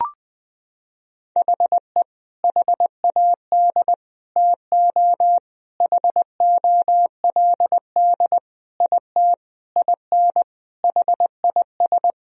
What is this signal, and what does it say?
Morse code, heard as a keyed tone, about 20 wpm, 700 Hz. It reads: HE HAD TO HOLD IT IN HIS
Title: Sentences from Top 1000 Words - Review Part 2 - 20wpm